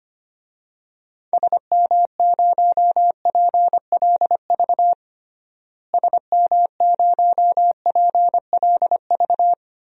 Morse code, heard as a keyed tone, about 25 wpm, 700 Hz. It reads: SM0PLV SM0PLV
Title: DXCC-100 Call Signs - 6 Characters - Repeated 2x - 25wpm